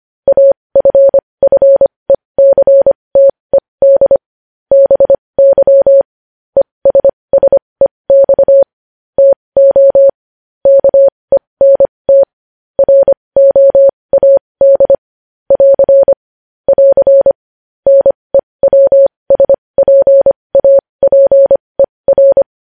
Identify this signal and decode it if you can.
Morse code, heard as a keyed tone, about 25 wpm, spaced as for 20 wpm, 550 Hz. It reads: AFFECTED BY ESSEX TO KENT ROAD <AR> <AR> NEWSPAPER
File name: podcast / News headlines-1x-25x20